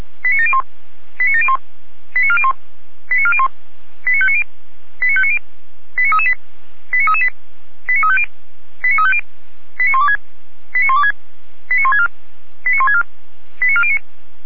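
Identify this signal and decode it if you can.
Five-tone selective call (ZVEI), heard as quick runs of five tones, tones of about 70 ms. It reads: 89701 89701 89401 89401 89590 89590 89338 89338 89350 89350 89126 89126 89164 89164 89409
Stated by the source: ZVEI five-tone selective call received on an SDR.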